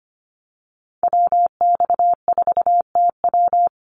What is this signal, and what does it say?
Morse code, heard as a keyed tone, about 25 wpm, 700 Hz. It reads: WX4TW
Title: NA SOTA Activator Callsigns 25wpm